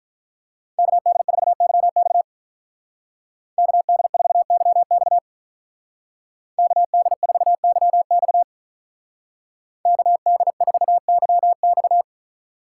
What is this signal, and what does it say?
Morse code, heard as a keyed tone, about 45 wpm, 700 Hz. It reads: KD4YX KD4YX KD4YX KD4YX